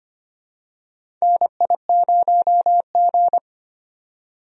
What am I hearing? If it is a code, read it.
Morse code, heard as a keyed tone, about 25 wpm, 700 Hz. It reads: NI0G
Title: USA Call Signs - 4 Characters - 25wpm